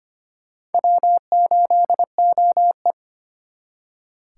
Morse code, heard as a keyed tone, about 33 wpm, 700 Hz. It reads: W8OE